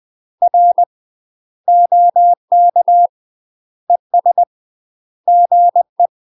Morse code, heard as a keyed tone, about 20 wpm, 700 Hz. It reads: R OK ES GE
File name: UK MASTER QSOs TRAINING / 20WPM / UK MASTER QSOs 0024